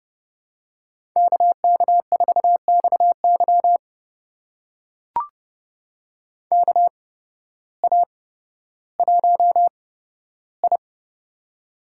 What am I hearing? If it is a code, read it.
Morse code, heard as a keyed tone, about 30 wpm, spaced as for 14 wpm, 700 Hz. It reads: KK4XY KA1I
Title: USA Call Signs - 8x Character Spacing - 30wpm